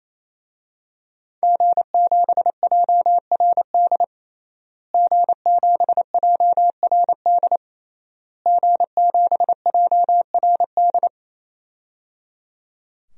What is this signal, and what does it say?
Morse code, heard as a keyed tone, about 28 wpm, 700 Hz. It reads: G7JRD G7JRD G7JRD
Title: DXCC-100 Call Signs - 5 Characters - Repeated 3x - 28wpm